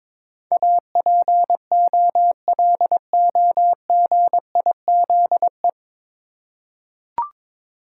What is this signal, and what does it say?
Morse code, heard as a keyed tone, about 22 wpm, 700 Hz. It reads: APOLOGIZE
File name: 9 Letter Words - First 3 Letters Given - 22wpm